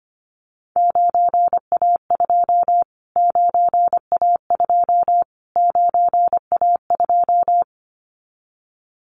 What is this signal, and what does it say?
Morse code, heard as a keyed tone, about 25 wpm, 700 Hz. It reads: 9A2 9A2 9A2
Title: Callsign Prefixes for Top 25 DXCC - Repeated 3x - 25wpm